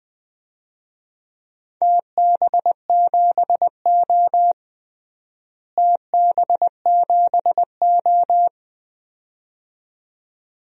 Morse code, heard as a keyed tone, about 21 wpm, 700 Hz. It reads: TB7O TB7O
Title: DXCC-100 Call Signs - 4 Characters - Repeated 2x - 20wpm